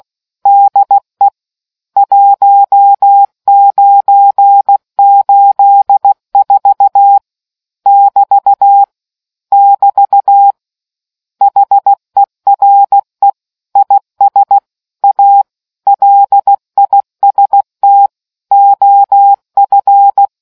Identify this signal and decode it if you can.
Morse code, heard as a keyed tone, about 16 wpm, 800 Hz. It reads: DE 1984 <BT> <BT> HERE IS A LIST OF